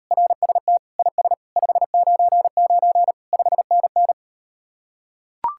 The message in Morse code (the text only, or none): RST IS 599 5NN